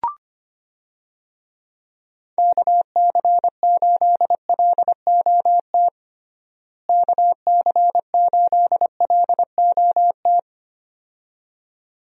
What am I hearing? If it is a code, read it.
Morse code, heard as a keyed tone, about 25 wpm, 700 Hz. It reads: KC8LOT KC8LOT
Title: USA Call Signs - 6 Characters - Repeated 2x - 25wpm